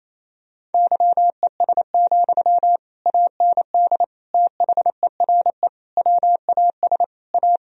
Morse code, heard as a keyed tone, about 28 wpm, 700 Hz. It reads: YES, AND THERE WAS A
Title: Sentences from Top 1000 Words - Review Part 3 - 28wpm